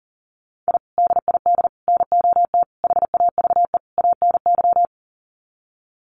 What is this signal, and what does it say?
Morse code, heard as a keyed tone, about 40 wpm, 700 Hz. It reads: I DID NOT HAVE ANY
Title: Sets of 5 Words Encore 40wpm